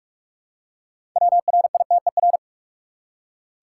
Morse code, heard as a keyed tone, about 45 wpm, 700 Hz. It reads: WRITER
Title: Six Letter Words - 45wpm